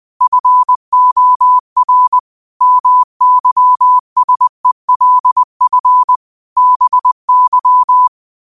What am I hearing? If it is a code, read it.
Morse code, heard as a keyed tone, about 20 wpm, 1000 Hz. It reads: FOR MYSELF BY
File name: qotdfiles / 20wpm